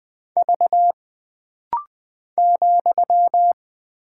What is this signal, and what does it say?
Morse code, heard as a keyed tone, about 20 wpm, 700 Hz. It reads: V ,
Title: Single Character (Letter:Number:Punctuation:Prosign) - Rapid-Fire - 20wpm